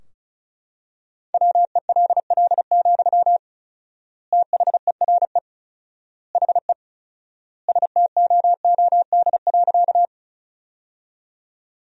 Morse code, heard as a keyed tone, about 34 wpm, 700 Hz. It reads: WELL, THERE HE STOOD.
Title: 4x Word-Spacing - Sentences from Top 500 Words - 35wpm